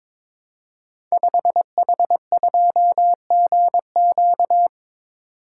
Morse code, heard as a keyed tone, about 22 wpm, 700 Hz. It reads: 5H2GQ
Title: Code Groups - 5 Letter:Numbers - 22wpm